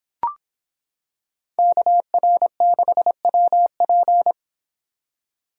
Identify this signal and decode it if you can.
Morse code, heard as a keyed tone, about 26 wpm, 700 Hz. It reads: KR6WP